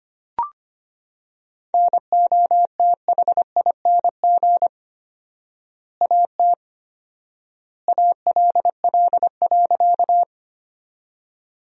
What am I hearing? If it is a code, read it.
Morse code, heard as a keyed tone, about 25 wpm, 700 Hz. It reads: NOTHING AT ALL.
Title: 4x Word-Spacing - Sentences from Top 500 Words - 25wpm